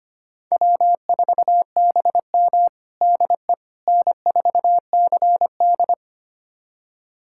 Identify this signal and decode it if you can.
Morse code, heard as a keyed tone, about 25 wpm, 700 Hz. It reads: W4BM DE N4CD
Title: USA Callsign DE Callsign - 4 Characters - 25wpm